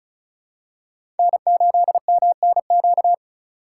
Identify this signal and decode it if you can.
Morse code, heard as a keyed tone, about 34 wpm, 700 Hz. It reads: N8MNQ
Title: USA Call Signs - 5 Characters - 4x Character Spacing - 35wpm